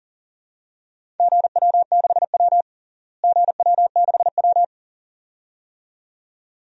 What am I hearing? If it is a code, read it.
Morse code, heard as a keyed tone, about 40 wpm, 700 Hz. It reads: GW6W GW6W
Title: DXCC-100 Call Signs - 4 Characters - Repeated 2x - 40wpm